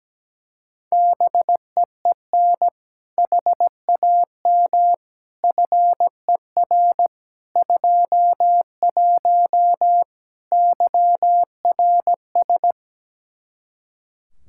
Morse code, heard as a keyed tone, about 17 wpm, 700 Hz. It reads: BEEN HAM FER 21 YRS
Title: Practice QSO - Exchange 3 - More Info - 17wpm